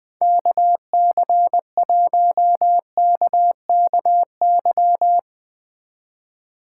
Morse code, heard as a keyed tone, about 20 wpm, 700 Hz. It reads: KC1KKY
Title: Speed Racing - USA 2x3 Call Signs 20wpm